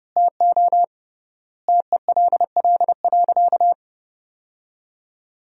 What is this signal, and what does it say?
Morse code, heard as a keyed tone, about 32 wpm, 700 Hz. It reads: TO TELL.
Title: 3x Word-Spacing - Sentences from Top 200 Words - 30wpm